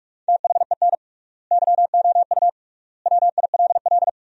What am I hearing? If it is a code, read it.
Morse code, heard as a keyed tone, about 46 wpm, 700 Hz. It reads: THEN YOU WILL
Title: 3x Word-Spacing - Sentences from Top 400 Words - 45wpm